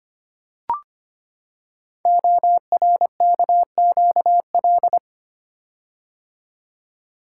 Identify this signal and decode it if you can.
Morse code, heard as a keyed tone, about 25 wpm, 700 Hz. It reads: ORKQL
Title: Code Groups - 5 Letters - 25wpm